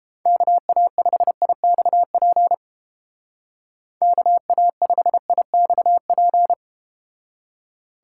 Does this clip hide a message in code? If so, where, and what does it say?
Yes: Morse code from 0.3 to 6.5 s, heard as a keyed tone, about 31 wpm, 700 Hz. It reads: KA5IXP KA5IXP